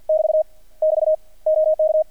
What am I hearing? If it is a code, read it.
Morse code, heard as a keyed tone, about 48 wpm, 630 Hz. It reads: <BT> <BT> OK
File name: archived / CRSnet